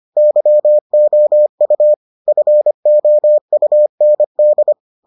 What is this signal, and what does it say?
Morse code, heard as a keyed tone, about 25 wpm, 600 Hz. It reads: YOU FOUND